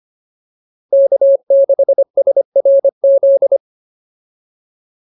Morse code, heard as a keyed tone, about 25 wpm, 550 Hz. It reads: K6SRZ